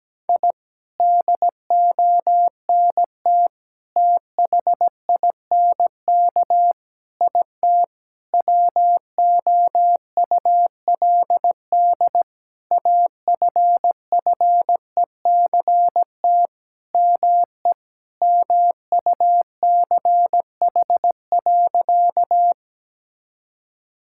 Morse code, heard as a keyed tone, about 17 wpm, 700 Hz. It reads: I DONT THINK IT WOULD AFFECT ME MUCH.
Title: Sentences from Top 1000 Words - Review Part 6 - 17wpm